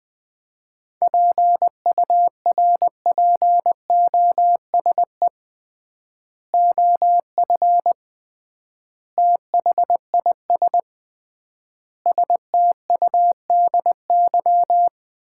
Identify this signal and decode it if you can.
Morse code, heard as a keyed tone, about 21 wpm, 700 Hz. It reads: PURPOSE OF THIS STUDY
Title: 3x Word-Spacing - Sets of 5 Words - 20wpm